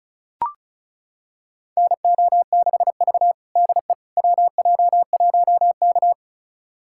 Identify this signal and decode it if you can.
Morse code, heard as a keyed tone, about 36 wpm, 690 Hz. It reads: NO6V DE WJ1K
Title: USA Callsign DE Callsign - 4 Characters - 35wpm